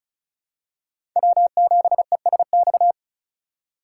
Morse code, heard as a keyed tone, about 34 wpm, 700 Hz. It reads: W7ESX
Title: USA Call Signs - 5 Characters - 3x Character Spacing - 35wpm